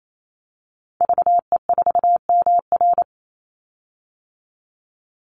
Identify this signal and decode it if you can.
Morse code, heard as a keyed tone, about 28 wpm, 700 Hz. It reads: VE4MR
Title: Call Signs - 28wpm